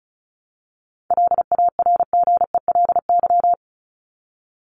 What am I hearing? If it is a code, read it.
Morse code, heard as a keyed tone, about 34 wpm, 700 Hz. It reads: LARGELY
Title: LY Suffix - Top 100 Words with Base Words 35wpm